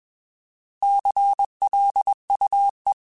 Morse code, heard as a keyed tone, about 21 wpm, 800 Hz. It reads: CLUE
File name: Alphabetical / Alphabetical mp3s / Cs @20